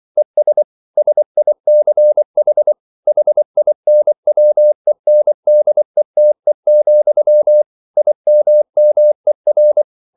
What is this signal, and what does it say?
Morse code, heard as a keyed tone, about 24 wpm, 600 Hz. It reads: ES SICH HINWENDETE, IMMER